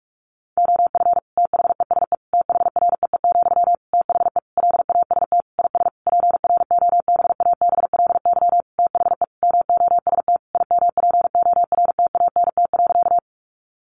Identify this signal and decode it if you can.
Morse code, heard as a keyed tone, about 44 wpm, 700 Hz. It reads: OF THESE THREE, THE LAST IS PROBABLY THE MOST IMPORTANT.
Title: Sentences from Top 700 Words 45wpm